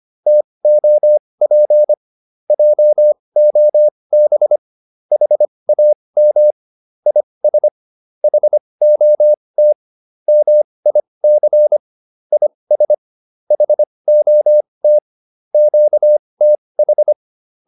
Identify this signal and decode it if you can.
Morse code, heard as a keyed tone, about 25 wpm, spaced as for 20 wpm, 600 Hz. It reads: TOP JOB HAM IS HOT MIC IS HOT QTH